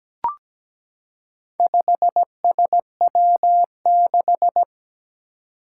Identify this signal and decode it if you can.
Morse code, heard as a keyed tone, about 17 wpm, 700 Hz. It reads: 5SW6
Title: Four Challenging Letter-Number Combinations 17wpm